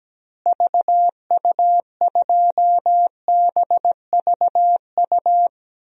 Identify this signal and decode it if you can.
Morse code, heard as a keyed tone, about 17 wpm, 700 Hz. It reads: VU2BVU
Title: DXCC-100 Call Signs - 6 Characters - Repeated 3x - 17wpm